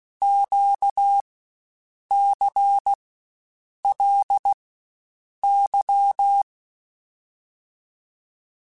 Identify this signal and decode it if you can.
Morse code, heard as a keyed tone, about 16 wpm, 800 Hz. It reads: Q C L Y
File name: Basic Letter Groups By7Groups / LFCYQ / LFCYQ@13